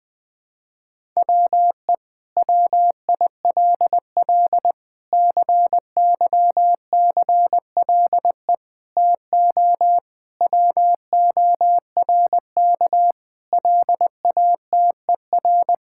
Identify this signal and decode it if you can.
Morse code, heard as a keyed tone, about 21 wpm, 700 Hz. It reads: WE WILL CYCLE TO WORK LATER
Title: Sentences from Top 1000 Words - Review Part 6 - 20wpm